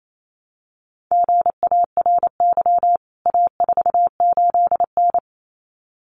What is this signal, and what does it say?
Morse code, heard as a keyed tone, about 28 wpm, 700 Hz. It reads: GARY A48N
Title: CWT Exchange 28wpm